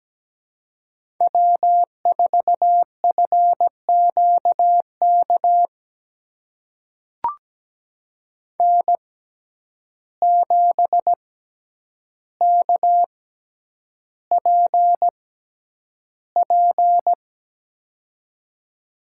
Morse code, heard as a keyed tone, about 17 wpm, spaced as for 9 wpm, 700 Hz. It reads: W4FQK N7KPP
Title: USA Call Signs - 5 Characters - 6x Character Spacing - 17wpm